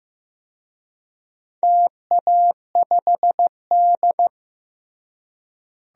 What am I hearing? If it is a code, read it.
Morse code, heard as a keyed tone, about 15 wpm, 700 Hz. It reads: TA5D
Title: DXCC-100 Call Signs - 4 Characters - Repeated 2x - 15wpm